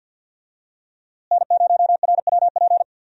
Morse code, heard as a keyed tone, about 46 wpm, 700 Hz. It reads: N0RWP
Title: USA Call Signs - 5 Characters - Repeated 2x - 50wpm